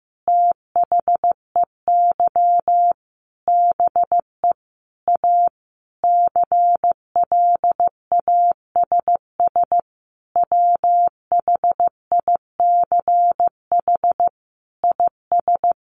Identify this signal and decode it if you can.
Morse code, heard as a keyed tone, about 15 wpm, 700 Hz. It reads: THEY BE A CLASS WHICH IS
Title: Sentences from Top 1400 Words 15wpm